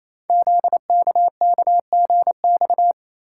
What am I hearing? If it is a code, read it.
Morse code, heard as a keyed tone, about 27 wpm, 700 Hz. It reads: ZKKGX
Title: Code Groups - 5 Letter:Numbers - 28wpm